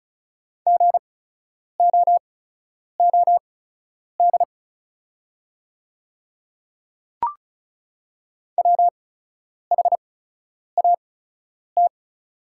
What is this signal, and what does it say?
Morse code, heard as a keyed tone, about 36 wpm, spaced as for 10 wpm, 690 Hz. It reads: GOOD WHAT
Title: 8x Character Spacing - Top 200 Words - 35wpm